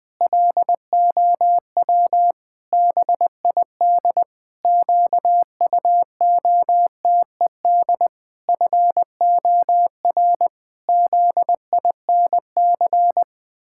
Morse code, heard as a keyed tone, about 21 wpm, 700 Hz. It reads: LOW BID QUOTED FOR ZINC
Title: Pangram Challenge v2 - 20wpm